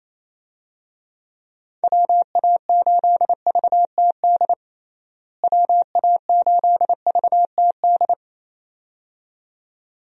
Morse code, heard as a keyed tone, about 28 wpm, 700 Hz. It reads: WA8VTD WA8VTD
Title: USA Call Signs - 6 Characters - Repeated 2x - 28wpm